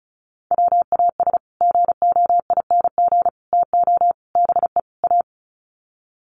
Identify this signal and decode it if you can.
Morse code, heard as a keyed tone, about 34 wpm, 700 Hz. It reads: WAS GOING TO BE A
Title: Sets of 5 Words 35wpm